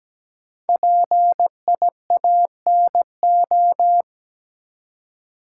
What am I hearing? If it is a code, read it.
Morse code, heard as a keyed tone, about 17 wpm, 700 Hz. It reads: PIANO
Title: Five Letter Words - 17wpm